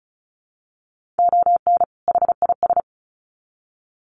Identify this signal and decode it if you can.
Morse code, heard as a keyed tone, about 34 wpm, 700 Hz. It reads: ON HIS
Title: Sets of 2 Words 35wpm